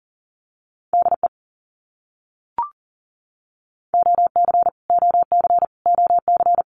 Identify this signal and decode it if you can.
Morse code, heard as a keyed tone, about 40 wpm, 700 Hz. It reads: DE OC OC OC
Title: Abbreviations Repeated 3x - 40wpm